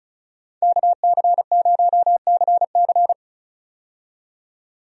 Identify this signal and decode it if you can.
Morse code, heard as a keyed tone, about 36 wpm, 700 Hz. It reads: KC0CC
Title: USA Call Signs - 5 Characters - 35wpm